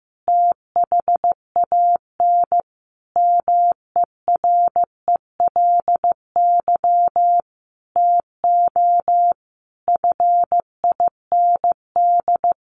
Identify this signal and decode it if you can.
Morse code, heard as a keyed tone, about 14 wpm, 700 Hz. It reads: THAN MERELY TO FIND
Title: Sentences from Top 1400 Words 15wpm